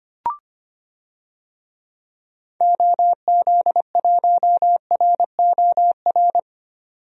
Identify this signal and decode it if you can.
Morse code, heard as a keyed tone, about 25 wpm, 700 Hz. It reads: OZ1ROR
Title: DXCC-100 Call Signs - 6 Characters - 25wpm